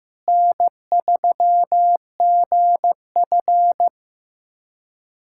Morse code, heard as a keyed tone, about 15 wpm, 700 Hz. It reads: N3GF
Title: USA Callsign DE Callsign - 4 Characters - 15wpm